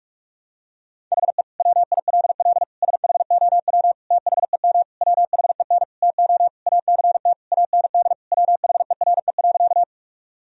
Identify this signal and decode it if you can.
Morse code, heard as a keyed tone, about 42 wpm, 700 Hz. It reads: HE WILL SHOW THEM WHEN TO ACT AND WHERE.
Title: Sentences from Top 1000 Words - Review Part 1 - 45wpm